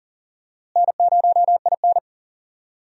Morse code, heard as a keyed tone, about 40 wpm, 700 Hz. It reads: N0IN
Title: USA Call Signs - 4 Characters - Repeated 2x - 40wpm